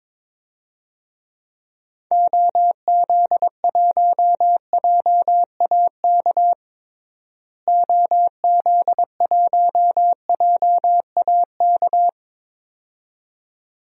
Morse code, heard as a keyed tone, about 22 wpm, 700 Hz. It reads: OZ1JAK OZ1JAK
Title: DXCC-100 Call Signs - 6 Characters - Repeated 2x - 22wpm